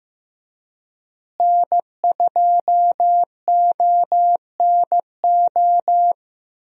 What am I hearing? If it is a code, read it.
Morse code, heard as a keyed tone, about 16 wpm, 700 Hz. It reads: N2ONO
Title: USA Call Signs - 5 Characters - Repeated 3x - 15wpm